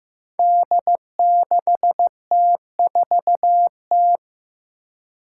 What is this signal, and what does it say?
Morse code, heard as a keyed tone, about 15 wpm, 700 Hz. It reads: D6T4T
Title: Five Challenging Letter-Number Combinations 15wpm